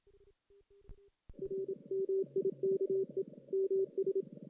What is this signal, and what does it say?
Morse code, heard as a keyed tone, about 27 wpm, 390 Hz. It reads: R MIKE MS
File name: 2016 JAN CW NAQP